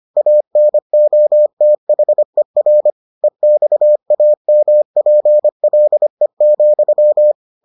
Morse code, heard as a keyed tone, about 25 wpm, 600 Hz. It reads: ANOTHER EXAMPLE,